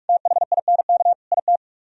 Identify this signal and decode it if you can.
Morse code, heard as a keyed tone, about 58 wpm, 700 Hz. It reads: THINK IT